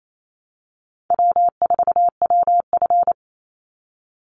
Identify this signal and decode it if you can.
Morse code, heard as a keyed tone, about 28 wpm, 700 Hz. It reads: W4WF
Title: Call Signs - 28wpm